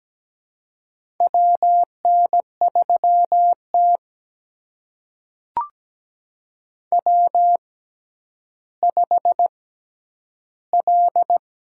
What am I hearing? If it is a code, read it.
Morse code, heard as a keyed tone, about 17 wpm, spaced as for 10 wpm, 700 Hz. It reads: WN3T W5L
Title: USA Call Signs - 4 Characters - 6x Character Spacing - 17wpm